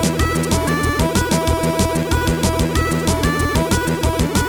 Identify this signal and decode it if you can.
There is no code.